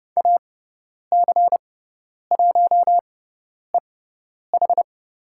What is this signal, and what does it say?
Morse code, heard as a keyed tone, about 30 wpm, spaced as for 10 wpm, 700 Hz. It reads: AC1EH
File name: Lesson 18g - Letter m - Call Signs 30wpm